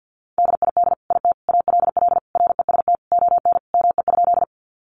Morse code, heard as a keyed tone, about 50 wpm, 700 Hz. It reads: DID IT ALL REST ON ME?
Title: Sentences from Top 500 Words 50wpm